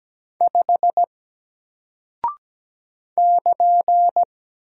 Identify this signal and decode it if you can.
Morse code, heard as a keyed tone, about 17 wpm, 700 Hz. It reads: 5 <KN>
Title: Single Character (Letter:Number:Punctuation:Prosign) Alt 17wpm